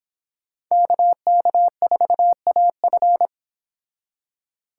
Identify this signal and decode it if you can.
Morse code, heard as a keyed tone, about 30 wpm, 700 Hz. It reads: KK4AF